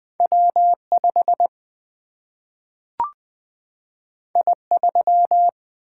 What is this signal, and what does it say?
Morse code, heard as a keyed tone, about 20 wpm, 700 Hz. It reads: W5 I3
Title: Single-Letter Single-Number 20wpm